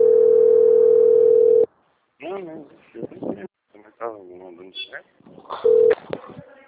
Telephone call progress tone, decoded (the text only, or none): ringback tone